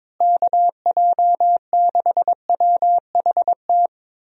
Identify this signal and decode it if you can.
Morse code, heard as a keyed tone, about 22 wpm, 700 Hz. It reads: KJ6WHT
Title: USA Call Signs - 6 Characters - Repeated 3x - 22wpm